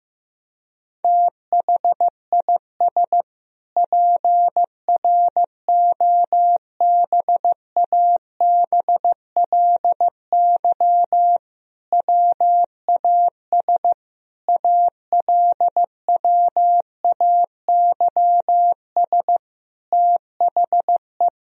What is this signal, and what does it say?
Morse code, heard as a keyed tone, about 15 wpm, 700 Hz. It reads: THIS PROBABLY WAS ALWAYS THE
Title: Sentences from Top 1000 Words - Review Part 4 - 15wpm